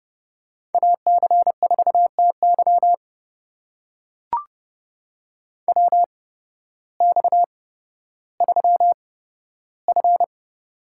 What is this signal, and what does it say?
Morse code, heard as a keyed tone, about 30 wpm, spaced as for 14 wpm, 700 Hz. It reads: AC4TY WX3F
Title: USA Call Signs - 8x Character Spacing - 30wpm